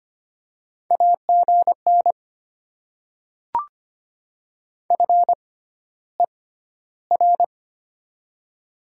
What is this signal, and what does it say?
Morse code, heard as a keyed tone, about 25 wpm, spaced as for 13 wpm, 700 Hz. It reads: AGN FER
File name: Top 20 QSO Elements - 6x Character Spacing - 25wpm